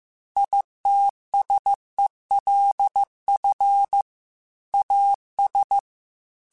Morse code, heard as a keyed tone, about 15 wpm, 800 Hz. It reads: ITSELF AS